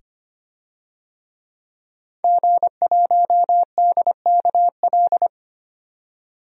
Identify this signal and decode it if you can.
Morse code, heard as a keyed tone, about 25 wpm, 700 Hz. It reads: G1DKL
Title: DXCC-100 Call Signs - 5 Characters - 25wpm